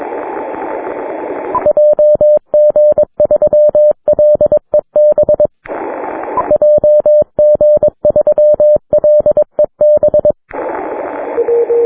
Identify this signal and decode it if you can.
Morse code, heard as a keyed tone, about 22 wpm, 600 Hz. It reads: JG3LEB JG3LEB
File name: wp / LA8AJA heard by JG3LEB on 6m